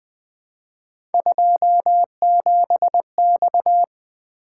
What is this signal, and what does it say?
Morse code, heard as a keyed tone, about 20 wpm, 700 Hz. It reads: 27X
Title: Three Letter-Number Combinations - 20wpm